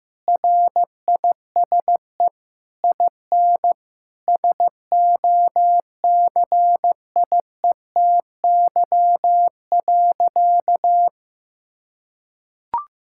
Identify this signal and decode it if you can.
Morse code, heard as a keyed tone, about 15 wpm, 700 Hz. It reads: RISE IN SOCIETY.
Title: Sentences from Top 1000 Words - Review Part 6 - 15wpm